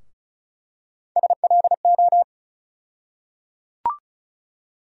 Morse code, heard as a keyed tone, about 34 wpm, 700 Hz. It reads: SLO
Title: USA Call Sign Postfixes - 3 Letters 35wpm